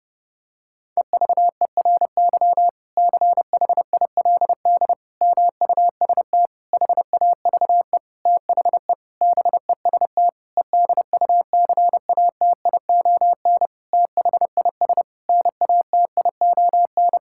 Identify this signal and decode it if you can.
Morse code, heard as a keyed tone, about 32 wpm, 700 Hz. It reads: EVERY CHILD MUST HAVE THE BEST EDUCATION THIS NATION
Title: Sentences from Top 1000 Words - Review Part 6 - 30wpm